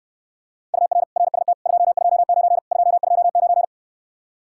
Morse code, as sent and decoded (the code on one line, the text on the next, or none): ..- .-. / .-. ... - / ..--- ..--- ----. / ..--- ..--- ----.
UR RST 229 229